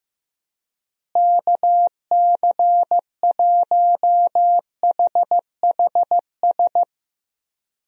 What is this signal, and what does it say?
Morse code, heard as a keyed tone, about 15 wpm, 700 Hz. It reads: KC1HHS
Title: USA Call Signs - 6 Characters - Repeated 3x - 15wpm